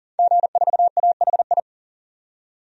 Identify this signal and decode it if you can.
Morse code, heard as a keyed tone, about 40 wpm, 700 Hz. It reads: G4AHI
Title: DXCC-100 Call Signs - 5 Characters - 40wpm